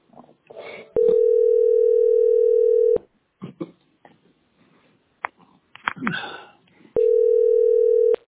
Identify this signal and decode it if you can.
Ringback tone (North America), 440 + 480 Hz.
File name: e -933d-4e88-b2ed